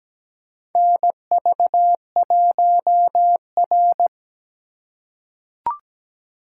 Morse code, heard as a keyed tone, about 17 wpm, 700 Hz. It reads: NV1R
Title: USA Call Signs - 2x Character Spacing - 17wpm